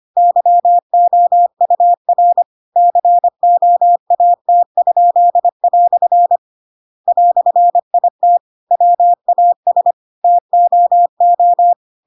Morse code, heard as a keyed tone, about 25 wpm, 700 Hz. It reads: YOUR COAT?" "IT WAS TOO
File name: Pinocchio / Pinocchio ch8 25 25 ik1hge